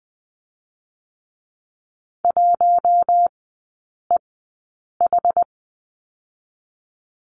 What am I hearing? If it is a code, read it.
Morse code, heard as a keyed tone, about 20 wpm, spaced as for 8 wpm, 700 Hz. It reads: 1EH
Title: 060 Lesson 11e - Letter h - Three Characters 20wpm